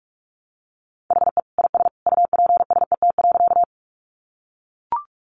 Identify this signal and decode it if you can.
Morse code, heard as a keyed tone, about 44 wpm, 700 Hz. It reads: HE IS UPSET.
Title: Additional Easy Phrases 45wpm